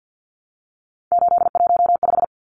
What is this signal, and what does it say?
Morse code, heard as a keyed tone, about 50 wpm, 700 Hz. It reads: 815
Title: Numbers - 3-Digit 50wpm